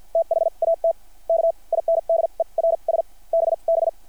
Morse code, heard as a keyed tone, about 50 wpm, spaced as for 38 wpm, 660 Hz. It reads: THAT KINDERS BB
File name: archived / CRSnet